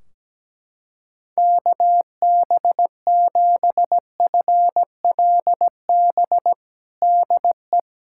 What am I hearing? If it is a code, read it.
Morse code, heard as a keyed tone, about 17 wpm, 700 Hz. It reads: KB7FLB DE
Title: Practice QSO - Exchange 4 - Ending - 17wpm